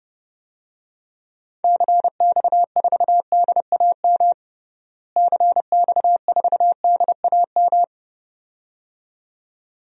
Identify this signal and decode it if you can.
Morse code, heard as a keyed tone, about 30 wpm, 700 Hz. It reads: CX4DAM CX4DAM
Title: DXCC-100 Call Signs - 6 Characters - Repeated 2x - 30wpm